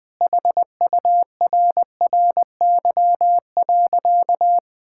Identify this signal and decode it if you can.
Morse code, heard as a keyed tone, about 21 wpm, 700 Hz. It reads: HURRY.
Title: Sentences with 4-Word Phrases - 20wpm